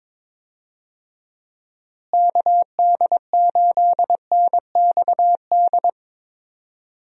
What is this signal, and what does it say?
Morse code, heard as a keyed tone, about 22 wpm, 700 Hz. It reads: KD8NXD
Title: USA Call Signs - 6 Characters - Repeated 3x - 22wpm